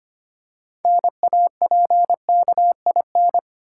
Morse code, heard as a keyed tone, about 25 wpm, 700 Hz. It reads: NAPKIN
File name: practice / 25wpm / Easy Words with Hints - 25wpm